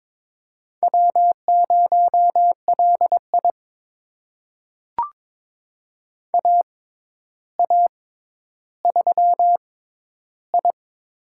Morse code, heard as a keyed tone, about 23 wpm, spaced as for 11 wpm, 700 Hz. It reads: W0LI AA3I
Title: USA Call Signs - 4 Characters - 6x Character Spacing - 22wpm